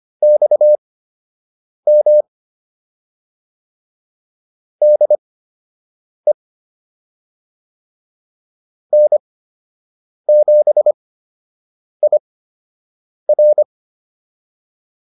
Morse code, heard as a keyed tone, about 25 wpm, spaced as for 7 wpm, 600 Hz. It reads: XM DE N7IR
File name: QSO 103 07